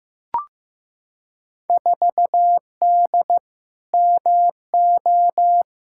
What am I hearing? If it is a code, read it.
Morse code, heard as a keyed tone, about 15 wpm, 700 Hz. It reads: 4D MO
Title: Field Day Exchanges Encore - 15wpm